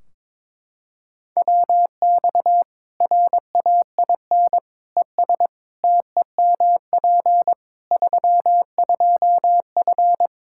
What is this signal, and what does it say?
Morse code, heard as a keyed tone, about 22 wpm, 700 Hz. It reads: WX RAIN ES TEMP 32F
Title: K1USN SST QSO Practice 22wpm